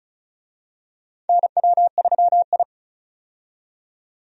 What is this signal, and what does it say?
Morse code, heard as a keyed tone, about 36 wpm, 700 Hz. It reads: NW3I
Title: USA Call Signs - 4 Characters - 35wpm